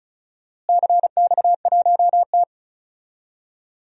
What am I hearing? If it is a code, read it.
Morse code, heard as a keyed tone, about 34 wpm, 700 Hz. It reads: CX1T
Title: DXCC-100 Call Signs - 4 Characters - Repeated 2x - 35wpm